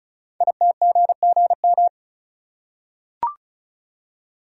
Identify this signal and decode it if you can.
Morse code, heard as a keyed tone, about 36 wpm, 690 Hz. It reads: ITGGM
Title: Code Groups - 5 Letters - 35wpm